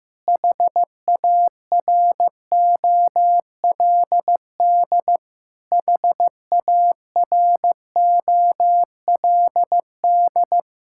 Morse code, heard as a keyed tone, about 15 wpm, 700 Hz. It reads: HAROLD HAROLD
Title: K1USN SST QSO Practice 15wpm